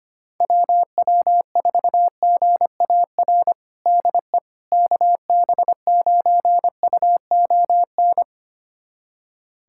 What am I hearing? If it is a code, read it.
Morse code, heard as a keyed tone, about 25 wpm, 700 Hz. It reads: WW4GAR DE KB9UON
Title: USA Callsign DE Callsign - 6 Characters - 25wpm